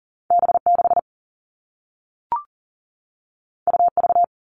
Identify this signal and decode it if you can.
Morse code, heard as a keyed tone, about 40 wpm, 700 Hz. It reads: B6 UV
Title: All Cut Numbers - 2-Digit 40wpm